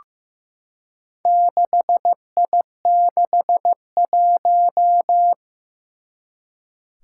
Morse code, heard as a keyed tone, about 15 wpm, 700 Hz. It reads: 6I61
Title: Four Challenging Letter-Number Combinations 15wpm